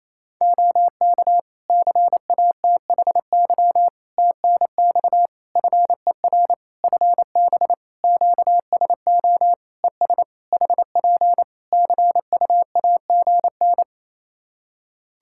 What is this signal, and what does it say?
Morse code, heard as a keyed tone, about 28 wpm, 700 Hz. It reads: OK CATHY TNX FER FB QSO ES HP CUAGN
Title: K1USN SST QSO Practice 28wpm